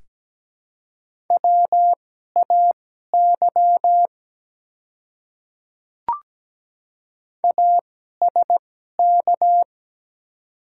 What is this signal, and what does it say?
Morse code, heard as a keyed tone, about 17 wpm, spaced as for 12 wpm, 700 Hz. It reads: WAY ASK
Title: 2x Character Spacing - Top 300 Words - 17wpm